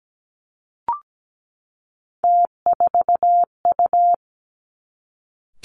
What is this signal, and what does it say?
Morse code, heard as a keyed tone, about 17 wpm, 700 Hz. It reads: T4U
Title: Three Challenging Letter-Number Combinations 17wpm